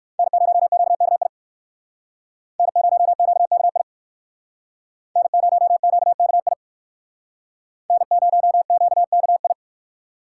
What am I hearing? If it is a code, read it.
Morse code, heard as a keyed tone, about 54 wpm, 700 Hz. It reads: N0QKI N0QKI N0QKI N0QKI
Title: Speed Racing - USA 1x3 Call Signs 45wpm